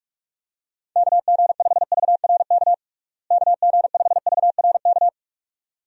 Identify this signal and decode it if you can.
Morse code, heard as a keyed tone, about 46 wpm, 700 Hz. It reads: KG5VRK KG5VRK
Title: USA Call Signs - 6 Characters - Repeated 2x - 45wpm